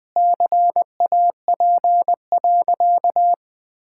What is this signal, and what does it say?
Morse code, heard as a keyed tone, about 20 wpm, 700 Hz. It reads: CAP.
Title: Sentences from Top 1600 Words 20wpm